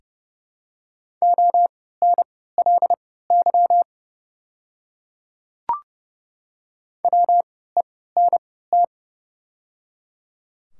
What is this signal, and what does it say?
Morse code, heard as a keyed tone, about 30 wpm, spaced as for 17 wpm, 690 Hz. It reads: ONLY WENT
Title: 3x Character Spacing - Top 300 Words - 30wpm